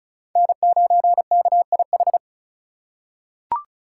Morse code, heard as a keyed tone, about 36 wpm, 700 Hz. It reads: N9KIH
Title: USA Call Signs - 5 Characters - 3x Character Spacing - 35wpm